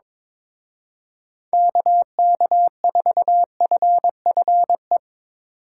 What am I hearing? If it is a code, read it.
Morse code, heard as a keyed tone, about 24 wpm, 700 Hz. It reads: KK4FFE
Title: Speed Racing - USA 2x3 Call Signs 22wpm